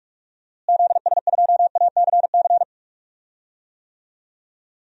Morse code, heard as a keyed tone, about 41 wpm, 700 Hz. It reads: ZS2ACC
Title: DXCC-100 Call Signs - 6 Characters - Repeated 3x - 45wpm